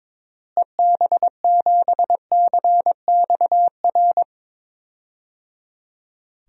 Morse code, heard as a keyed tone, about 22 wpm, 700 Hz. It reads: EB7CXR
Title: DXCC-100 Call Signs - 6 Characters - Repeated 3x - 22wpm